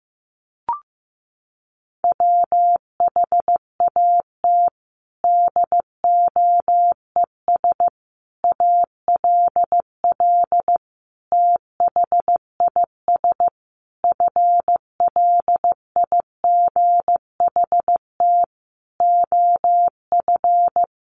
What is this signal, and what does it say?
Morse code, heard as a keyed tone, about 15 wpm, 700 Hz. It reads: WHAT DOES ALL THIS FLIGHT OF
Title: Sentences from Top 1300 Words 15wpm